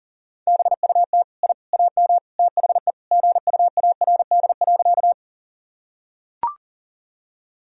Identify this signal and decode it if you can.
Morse code, heard as a keyed tone, about 40 wpm, 700 Hz. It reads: BUT I AM THE GUARD.
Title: Sentences from Top 1000 Words - Review Part 4 - 40wpm